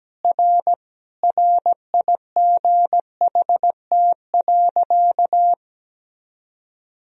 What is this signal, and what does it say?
Morse code, heard as a keyed tone, about 17 wpm, 700 Hz. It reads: R RIGHT.
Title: Sentences from Top 300 Words and Abbreviations 17wpm